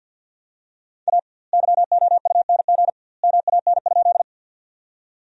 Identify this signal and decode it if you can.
Morse code, heard as a keyed tone, about 48 wpm, 700 Hz. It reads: A YOUNG MAN?
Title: 2x Word-Spacing - Sentences from Top 400 Words - 50wpm